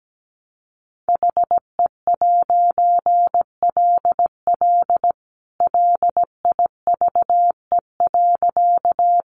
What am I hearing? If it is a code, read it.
Morse code, heard as a keyed tone, about 17 wpm, 700 Hz. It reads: HE'LL LIVE.
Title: A Sound Of Thunder 17wpm